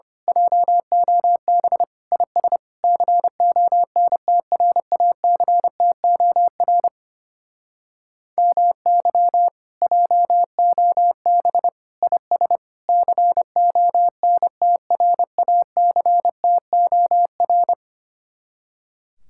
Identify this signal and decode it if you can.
Morse code, heard as a keyed tone, about 27 wpm, 700 Hz. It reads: JOB IS CONTRACTOR MY JOB IS CONTRACTOR
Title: Speed-Racing Common QSO Phrases - 25wpm